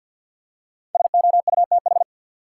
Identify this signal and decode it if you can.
Morse code, heard as a keyed tone, about 53 wpm, 700 Hz. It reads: SOUTH
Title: Five Letter Words - 50wpm